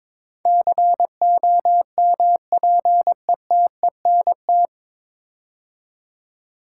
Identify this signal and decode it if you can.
Morse code, heard as a keyed tone, about 22 wpm, 700 Hz. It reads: COMPETENT
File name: 9 Letter Words - First 3 Letters Given - 22wpm